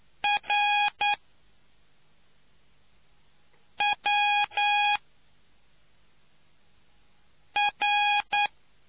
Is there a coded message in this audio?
Yes: Morse code from 0.2 to 8.5 s, heard as a keyed tone, about 9 wpm, 820 Hz. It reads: R W R